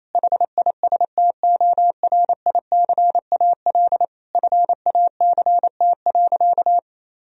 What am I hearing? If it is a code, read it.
Morse code, heard as a keyed tone, about 28 wpm, 700 Hz. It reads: HISTORICAL FACT.
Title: Sentences from Top 1000 Words - Review Part 4 - 28wpm